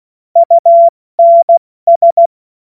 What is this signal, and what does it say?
Morse code, heard as a keyed tone, about 16 wpm, 680 Hz. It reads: UNS
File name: Morsekurs / Witze / Witz 17. Januar 2023 70 F80